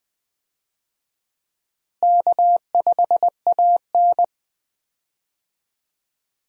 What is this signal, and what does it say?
Morse code, heard as a keyed tone, about 20 wpm, 700 Hz. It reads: K5AN
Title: USA Call Signs - 4 Characters - Repeated 2x - 20wpm